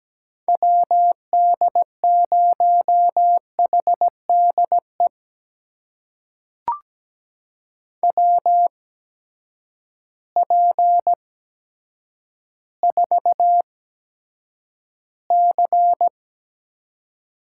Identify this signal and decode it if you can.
Morse code, heard as a keyed tone, about 17 wpm, spaced as for 8 wpm, 700 Hz. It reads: WD0HDE WP4C
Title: USA Call Signs - 6 Characters - 8x Character Spacing - 17wpm